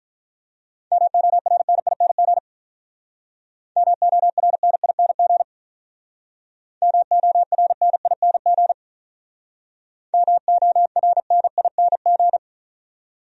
Morse code, heard as a keyed tone, about 43 wpm, 700 Hz. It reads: MORNING MORNING MORNING MORNING
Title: Speed-Racing Top 500 Words 35wpm